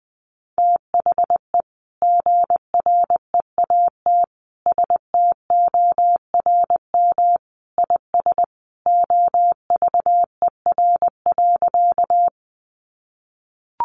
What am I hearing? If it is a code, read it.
Morse code, heard as a keyed tone, about 20 wpm, 700 Hz. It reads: THE GREAT STORM IS OVER.
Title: Sentences from Top 1500 Words 20wpm